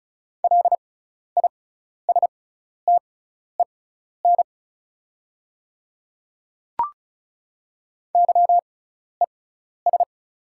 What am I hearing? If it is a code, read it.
Morse code, heard as a keyed tone, about 36 wpm, spaced as for 12 wpm, 690 Hz. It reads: LISTEN YES
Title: 6x Character Spacing - Top 500 Words - 35wpm